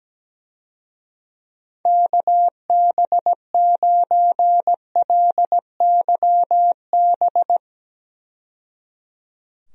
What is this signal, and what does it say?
Morse code, heard as a keyed tone, about 17 wpm, 700 Hz. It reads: KB9LYB
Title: USA Call Signs - 6 Characters - 17wpm